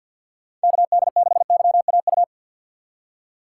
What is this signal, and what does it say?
Morse code, heard as a keyed tone, about 52 wpm, 700 Hz. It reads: KD6YAU
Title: USA Call Signs - 6x Character Spacing - 50wpm